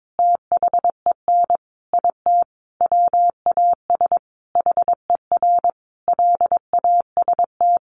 Morse code, heard as a keyed tone, about 22 wpm, 700 Hz. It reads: THEN IT WAS HER LAST
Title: Sentences from Top 1400 Words 22wpm